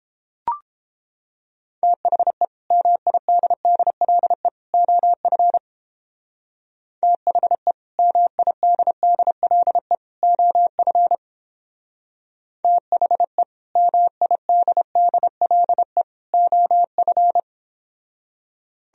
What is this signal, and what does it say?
Morse code, heard as a keyed tone, about 29 wpm, 700 Hz. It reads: THE MIDDLE OF THE MIDDLE OF THE MIDDLE OF